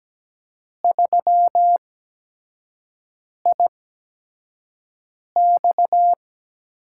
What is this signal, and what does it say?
Morse code, heard as a keyed tone, about 17 wpm, spaced as for 5 wpm, 700 Hz. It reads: 3IX
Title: USA Call Signs - 5 Characters - 8x Character Spacing - 17wpm